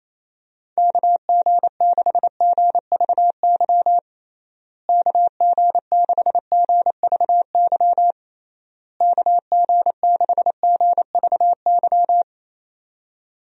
Morse code, heard as a keyed tone, about 28 wpm, 700 Hz. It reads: KG6GVY KG6GVY KG6GVY
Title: USA Call Signs - 6 Characters - Repeated 3x - 28wpm